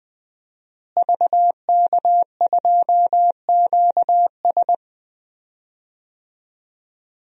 Morse code, heard as a keyed tone, about 20 wpm, 700 Hz. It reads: VK2QS
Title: DXCC-100 Call Signs - 5 Characters - Repeated 2x - 20wpm